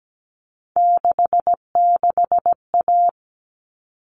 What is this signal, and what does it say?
Morse code, heard as a keyed tone, about 17 wpm, 700 Hz. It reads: 66A
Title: All Cut Numbers - 3-Digit 17wpm